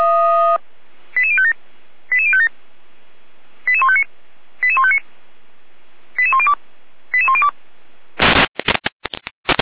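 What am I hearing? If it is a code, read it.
Five-tone selective call (ZVEI), heard as quick runs of five tones, tones of about 70 ms. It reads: 88057 88057 88159 88159 88102 88102
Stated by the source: ZVEI five-tone selective call received by SDR.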